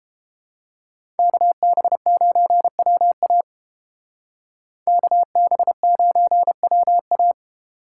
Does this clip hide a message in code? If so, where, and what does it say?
Yes: Morse code from 1.2 to 7.3 s, heard as a keyed tone, about 30 wpm, 700 Hz. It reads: KB9WA KB9WA